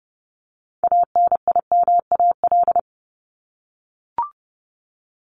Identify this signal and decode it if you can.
Morse code, heard as a keyed tone, about 30 wpm, 700 Hz. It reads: ANIMAL
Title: ES:S Suffix - Top 100 Words with Base Words 30wpm